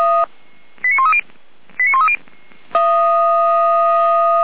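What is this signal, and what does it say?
Five-tone selective call (ZVEI), heard as quick runs of five tones, tones of about 70 ms. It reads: 89130 89130
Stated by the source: ZVEI five-tone selective call received by SDR, logged on the audio server of Feuerwehr Orlamünde fire brigade.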